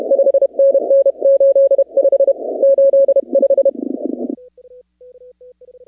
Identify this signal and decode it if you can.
Morse code, heard as a keyed tone, about 32 wpm, 550 Hz. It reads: 5NN 85 85
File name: contests / 2019-cw-wpx-kt5j-k5pi- -2330 0-15